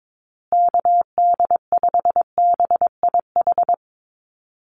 Morse code, heard as a keyed tone, about 22 wpm, 700 Hz. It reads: KD5BIH
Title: USA Call Signs Repeated 3x 22wpm